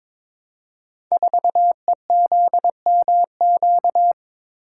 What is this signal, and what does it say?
Morse code, heard as a keyed tone, about 22 wpm, 700 Hz. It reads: 4EZMQ
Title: Code Groups - 5 Letter:Numbers - 22wpm